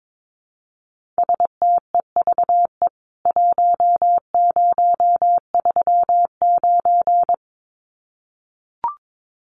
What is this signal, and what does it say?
Morse code, heard as a keyed tone, about 22 wpm, 700 Hz. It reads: STEVE 1039
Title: CWT Exchange 22wpm